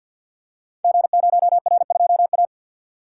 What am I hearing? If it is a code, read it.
Morse code, heard as a keyed tone, about 51 wpm, 700 Hz. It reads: G0R2A
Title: Code Groups - 5 Letter:Numbers - 50wpm